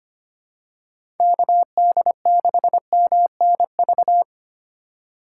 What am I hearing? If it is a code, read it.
Morse code, heard as a keyed tone, about 25 wpm, 700 Hz. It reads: KD6MNV
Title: USA Call Signs - 6 Characters - 3x Character Spacing - 25wpm